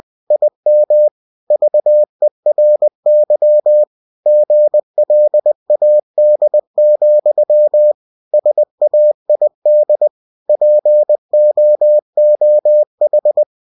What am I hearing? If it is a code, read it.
Morse code, heard as a keyed tone, about 20 wpm, 600 Hz. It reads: IM VERY GLAD, SAID POOH
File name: cw-books / winnie-the-pooh / winnie-20wpm-page-0058